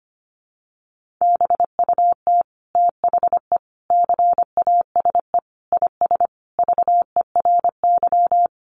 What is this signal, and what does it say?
Morse code, heard as a keyed tone, about 25 wpm, 700 Hz. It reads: BUT THE CASE IS VERY
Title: Sentences from Top 700 Words 25wpm